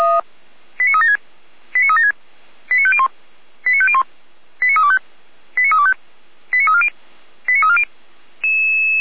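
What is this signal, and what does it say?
Five-tone selective call (ZVEI), heard as quick runs of five tones, tones of about 70 ms. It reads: 89376 89376 89501 89501 89325 89325 89340 89340
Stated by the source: Five-tone ZVEI selective call 89305, received on an SDR by fire brigade Feuerwehr Orlamünde.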